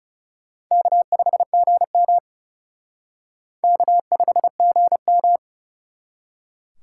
Morse code, heard as a keyed tone, about 36 wpm, 700 Hz. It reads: K5GM K5GM
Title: Speed Racing - USA 1x2 Call Signs 30wpm